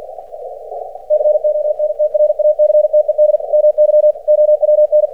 Morse code, heard as a keyed tone, about 50 wpm, 600 Hz. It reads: CONTRACTED MY OWN
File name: archived / CRSnet